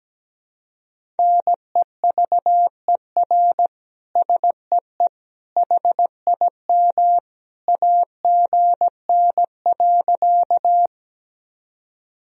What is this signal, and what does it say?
Morse code, heard as a keyed tone, about 17 wpm, 700 Hz. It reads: NEVER SEE HIM AGN.
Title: Sentences from Top 300 Words and Abbreviations 17wpm